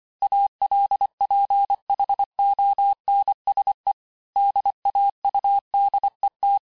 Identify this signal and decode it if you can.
Morse code, heard as a keyed tone, about 25 wpm, 800 Hz. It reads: ALPHONSE DAUDET
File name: cours-868-teste-clair-vitesse-23mpm